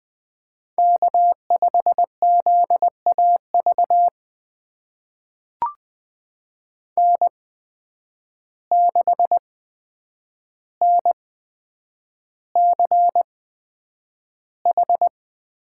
Morse code, heard as a keyed tone, about 20 wpm, spaced as for 9 wpm, 700 Hz. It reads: K5ZAV N6NCH
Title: USA Call Signs - 5 Characters - 8x Character Spacing - 20wpm